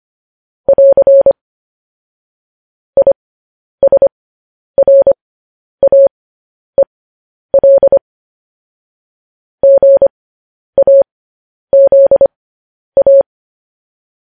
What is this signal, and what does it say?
Morse code, heard as a keyed tone, about 25 wpm, spaced as for 10 wpm, 550 Hz. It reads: <AR> ISRAEL GAZA